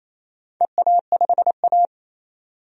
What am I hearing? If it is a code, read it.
Morse code, heard as a keyed tone, about 29 wpm, 700 Hz. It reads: EA5A
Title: DXCC-100 Call Signs - 4 Characters - 28wpm